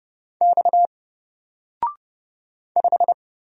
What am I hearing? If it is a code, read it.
Morse code, heard as a keyed tone, about 29 wpm, 700 Hz. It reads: X 5
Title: Single Letter-Number Alt 30wpm